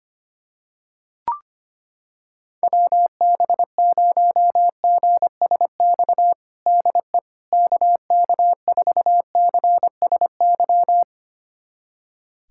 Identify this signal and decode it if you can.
Morse code, heard as a keyed tone, about 25 wpm, 700 Hz. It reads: WB0GSX DE KK4CSY
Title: USA Callsign DE Callsign - 6 Characters - 25wpm